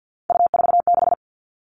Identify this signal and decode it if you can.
Morse code, heard as a keyed tone, about 50 wpm, 700 Hz. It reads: U46
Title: Three Challenging Letter-Number Combinations 50wpm